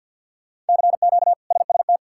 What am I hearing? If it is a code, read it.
Morse code, heard as a keyed tone, about 52 wpm, 690 Hz. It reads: CQ SST
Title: K1USN SST QSO Practice - 50wpm